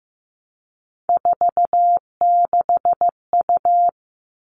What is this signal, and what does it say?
Morse code, heard as a keyed tone, about 15 wpm, 700 Hz. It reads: 46U
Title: All Cut Numbers - 3-Digit 15wpm